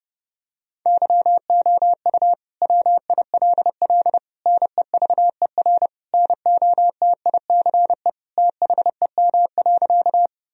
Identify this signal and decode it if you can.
Morse code, heard as a keyed tone, about 32 wpm, 700 Hz. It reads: YOU WILL NEVER NOTICE THEM.
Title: Sentences from Top 1000 Words - Review Part 2 - 30wpm